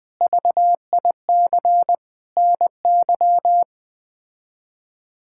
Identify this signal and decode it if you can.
Morse code, heard as a keyed tone, about 21 wpm, 700 Hz. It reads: VIC NY
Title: K1USN SST QSO Practice - 20wpm